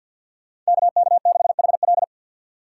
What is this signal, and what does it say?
Morse code, heard as a keyed tone, about 62 wpm, 700 Hz. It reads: KK6HL
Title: Speed Racing - USA 2x2 Call Signs 50wpm